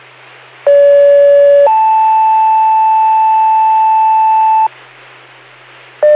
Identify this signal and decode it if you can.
Two-tone page, heard as a one-second tone, then a three-second tone, about 570 then 880 Hz. It is heard from 0.7 to 4.7 s.